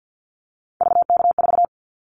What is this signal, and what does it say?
Morse code, heard as a keyed tone, about 50 wpm, 700 Hz. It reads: VK4
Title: Callsign Prefixes for Top 100 DXCC - 50wpm